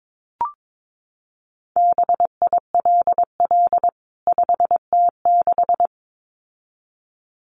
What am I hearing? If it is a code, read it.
Morse code, heard as a keyed tone, about 22 wpm, 700 Hz. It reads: BILL 5T6
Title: CWT Exchange 22wpm